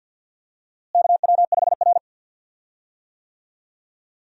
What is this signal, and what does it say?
Morse code, heard as a keyed tone, about 52 wpm, 700 Hz. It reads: KW5R
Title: USA Call Signs - 4 Characters - Repeated 3x - 50wpm